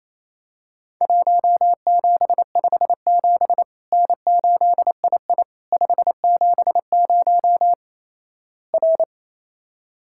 Morse code, heard as a keyed tone, about 28 wpm, 700 Hz. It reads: 1757 N8II 570 R
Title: Worked All Europe DX Contest - QTC Exchange - 28wpm